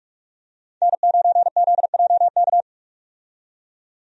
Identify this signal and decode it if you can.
Morse code, heard as a keyed tone, about 45 wpm, 700 Hz. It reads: N9ZJK
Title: USA Call Signs - 5 Characters - 45wpm